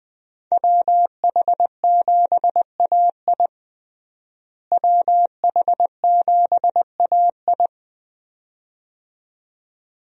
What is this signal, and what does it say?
Morse code, heard as a keyed tone, about 21 wpm, 700 Hz. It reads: WH7AI WH7AI
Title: USA Call Signs - 5 Characters - Repeated 2x - 20wpm